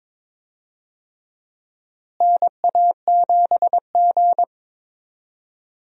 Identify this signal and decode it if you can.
Morse code, heard as a keyed tone, about 21 wpm, 700 Hz. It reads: NA7G
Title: USA Call Signs - 4 Characters - Repeated 3x - 22wpm